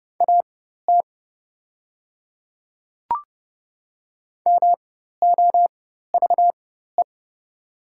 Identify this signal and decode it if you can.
Morse code, heard as a keyed tone, about 31 wpm, spaced as for 14 wpm, 690 Hz. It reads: AT MOVE
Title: 4x Character Spacing - Top 200 Words - 30wpm